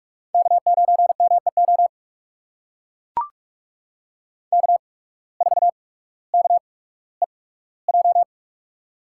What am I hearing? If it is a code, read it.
Morse code, heard as a keyed tone, about 43 wpm, spaced as for 19 wpm, 700 Hz. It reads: K9MEO K4KEJ
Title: USA Call Signs - 5 Characters - 8x Character Spacing - 45wpm